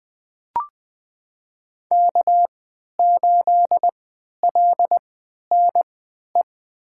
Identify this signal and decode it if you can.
Morse code, heard as a keyed tone, about 20 wpm, spaced as for 11 wpm, 700 Hz. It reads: K8LNE
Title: USA Call Signs - 5 Characters - 3x Character Spacing - 20wpm